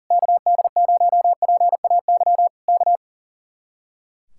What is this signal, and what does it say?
Morse code, heard as a keyed tone, about 37 wpm, 700 Hz. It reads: KD0PAY K
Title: Speed-Racing Common QSO Phrases - 40wpm